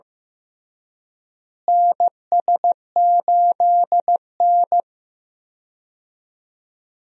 Morse code, heard as a keyed tone, about 16 wpm, 700 Hz. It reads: NS8N
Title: USA Call Signs - 4 Characters - Repeated 3x - 15wpm